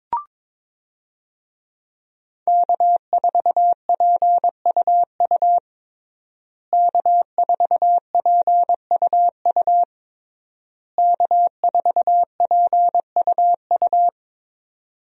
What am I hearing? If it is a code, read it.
Morse code, heard as a keyed tone, about 22 wpm, 700 Hz. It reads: K4PUU K4PUU K4PUU
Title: USA Call Signs - 5 Characters - Repeated 3x - 22wpm